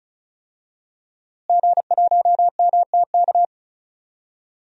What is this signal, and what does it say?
Morse code, heard as a keyed tone, about 35 wpm, 700 Hz. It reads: G1MTK
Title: DXCC-100 Call Signs - 5 Characters - Repeated 2x - 35wpm